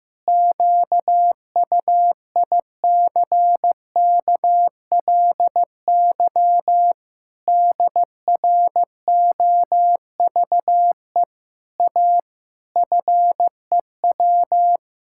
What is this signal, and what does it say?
Morse code, heard as a keyed tone, about 14 wpm, 700 Hz. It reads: QUICKLY DROVE A FEW
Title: Pangram Challenge v2 - 15wpm